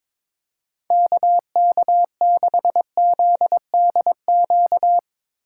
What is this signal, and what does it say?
Morse code, heard as a keyed tone, about 22 wpm, 700 Hz. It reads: KK6ZDQ
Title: USA Call Signs - 6 Characters - 3x Character Spacing - 22wpm